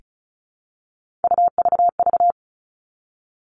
Morse code, heard as a keyed tone, about 34 wpm, 700 Hz. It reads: UVV
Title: All Cut Numbers - 3-Digit 35wpm